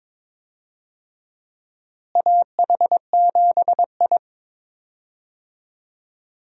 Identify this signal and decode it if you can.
Morse code, heard as a keyed tone, about 21 wpm, 700 Hz. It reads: AH7I
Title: USA Call Signs - 4 Characters - Repeated 3x - 22wpm